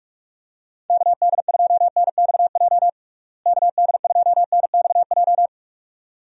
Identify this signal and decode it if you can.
Morse code, heard as a keyed tone, about 46 wpm, 700 Hz. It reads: KD2NXJ KD2NXJ
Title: USA Call Signs - 6 Characters - Repeated 2x - 45wpm